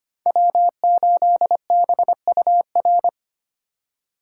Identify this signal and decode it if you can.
Morse code, heard as a keyed tone, about 25 wpm, 700 Hz. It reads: W8BUR
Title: USA Call Signs - 5 Characters - 2x Character Spacing - 25wpm